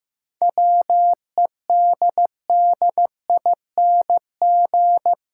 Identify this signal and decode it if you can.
Morse code, heard as a keyed tone, about 15 wpm, 700 Hz. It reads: WEDDING
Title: Seven Letter Words - 15wpm